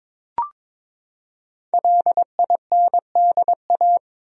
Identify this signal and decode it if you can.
Morse code, heard as a keyed tone, about 22 wpm, 700 Hz. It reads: LINDA
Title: USA Common Names - Max 5 Letters - 22wpm